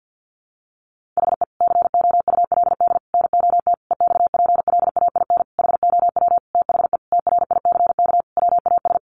Morse code, heard as a keyed tone, about 50 wpm, 700 Hz. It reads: HE COULD NOT EXPLAIN HOW THE TRICK WAS
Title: Sentences from Top 1500 Words 50wpm